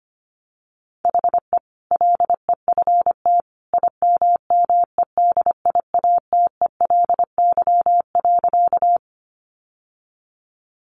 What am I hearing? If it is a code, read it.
Morse code, heard as a keyed tone, about 25 wpm, 700 Hz. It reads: HE LEFT IMMEDIATELY.
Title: Sentences from Top 1100 Words 25wpm